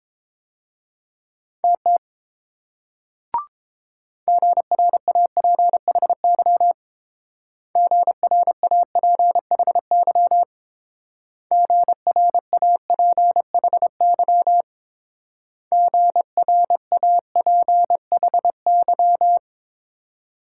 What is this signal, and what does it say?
Morse code, heard as a keyed tone, about 28 wpm, 700 Hz. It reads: TT GRAPHY GRAPHY GRAPHY GRAPHY